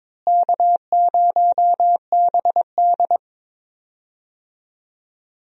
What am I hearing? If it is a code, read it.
Morse code, heard as a keyed tone, about 22 wpm, 700 Hz. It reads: K0BD
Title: USA Call Signs - 4 Characters - Repeated 2x - 22wpm